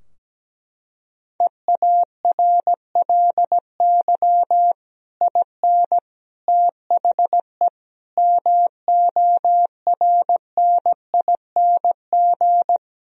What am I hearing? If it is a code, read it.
Morse code, heard as a keyed tone, about 17 wpm, 700 Hz. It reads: EARLY IN THE MORNING
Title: Sentences from Top 1000 Words - Review Part 2 - 17wpm